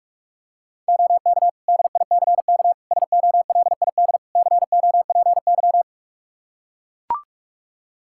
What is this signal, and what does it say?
Morse code, heard as a keyed tone, about 43 wpm, 700 Hz. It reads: OK DICK SOLID COPY
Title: Practice QSO - Exchange 3 - More Info - 45wpm